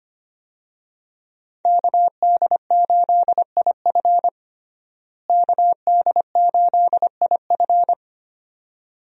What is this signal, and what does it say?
Morse code, heard as a keyed tone, about 25 wpm, 700 Hz. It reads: KD8IF KD8IF
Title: USA Call Signs - 5 Characters - Repeated 2x - 25wpm